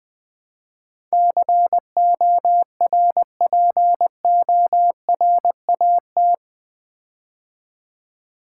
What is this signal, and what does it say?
Morse code, heard as a keyed tone, about 20 wpm, 700 Hz. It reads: CORPORAT
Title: Build A Word - 4 Letters At A Time - 11-Letter Words - 20wpm